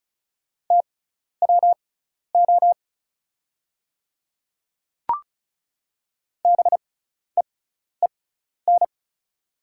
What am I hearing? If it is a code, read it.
Morse code, heard as a keyed tone, about 36 wpm, spaced as for 12 wpm, 690 Hz. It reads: TWO BEEN
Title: 6x Character Spacing - Top 300 Words - 35wpm